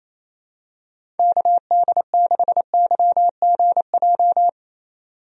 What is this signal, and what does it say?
Morse code, heard as a keyed tone, about 27 wpm, 700 Hz. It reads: KD6YGJ
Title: USA Call Signs - 6 Characters - 28wpm